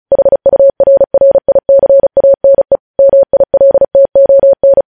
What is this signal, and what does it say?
Morse code, heard as a keyed tone, about 36 wpm, 550 Hz. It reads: HURRICANE MILTON